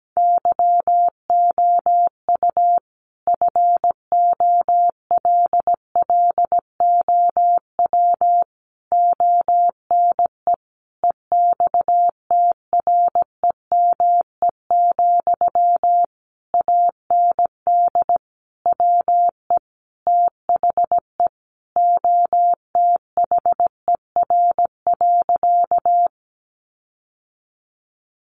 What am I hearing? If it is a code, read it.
Morse code, heard as a keyed tone, about 17 wpm, 700 Hz. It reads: YOU FOLLOW ONE EXTREME, AND WE THE OTHER.
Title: Sentences from Top 1300 Words 17wpm